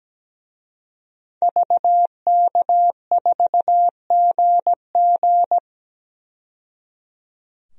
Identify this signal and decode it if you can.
Morse code, heard as a keyed tone, about 17 wpm, 700 Hz. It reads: VK4GG
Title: DXCC-100 Call Signs - 5 Characters - 17wpm